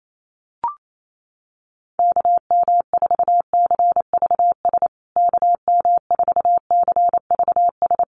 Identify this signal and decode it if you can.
Morse code, heard as a keyed tone, about 28 wpm, 690 Hz. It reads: KM4CVS KM4CVS
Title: USA Call Signs Repeated 3x 28wpm